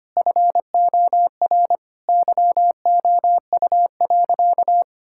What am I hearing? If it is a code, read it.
Morse code, heard as a keyed tone, about 25 wpm, 700 Hz. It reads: FOR YOU.
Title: Sentences from Top 1600 Words 25wpm